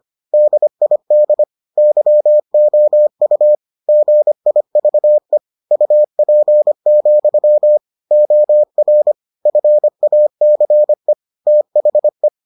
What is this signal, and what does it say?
Morse code, heard as a keyed tone, about 25 wpm, 600 Hz. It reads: DID YOU GIVE UP, OR FACE THE